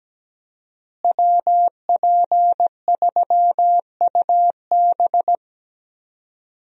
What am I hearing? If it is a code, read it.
Morse code, heard as a keyed tone, about 17 wpm, 700 Hz. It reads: WP3UB
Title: USA Call Signs - 5 Characters - Repeated 2x - 17wpm